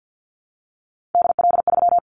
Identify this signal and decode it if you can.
Morse code, heard as a keyed tone, about 50 wpm, 700 Hz. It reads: DL3
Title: Callsign Prefixes for Top 50 DXCC - 50wpm